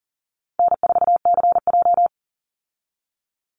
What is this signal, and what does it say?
Morse code, heard as a keyed tone, about 40 wpm, 700 Hz. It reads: N4CJ
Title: Easy Call Signs Repeated 2x - 40wpm